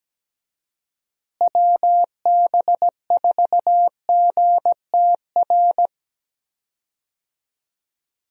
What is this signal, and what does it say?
Morse code, heard as a keyed tone, about 17 wpm, 700 Hz. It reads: WB4GTR
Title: USA Call Signs - 6 Characters - Repeated 3x - 17wpm